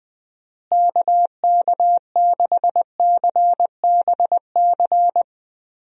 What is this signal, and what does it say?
Morse code, heard as a keyed tone, about 20 wpm, 700 Hz. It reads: KK6CBC
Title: USA Call Signs - 6 Characters - Repeated 3x - 20wpm